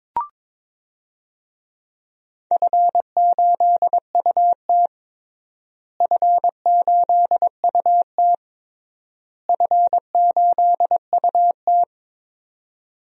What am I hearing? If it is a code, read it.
Morse code, heard as a keyed tone, about 22 wpm, 700 Hz. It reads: F8UT F8UT F8UT
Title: DXCC-100 Call Signs - 4 Characters - Repeated 3x - 22wpm